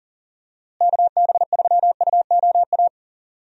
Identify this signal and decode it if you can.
Morse code, heard as a keyed tone, about 37 wpm, 700 Hz. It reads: KB3UOA
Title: USA Call Signs - 6 Characters - Repeated 2x - 40wpm